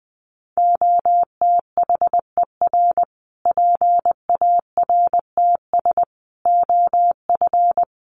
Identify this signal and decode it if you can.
Morse code, heard as a keyed tone, about 20 wpm, 700 Hz. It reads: OTHER PARTS OF
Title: Sets of 5 Words Encore 20wpm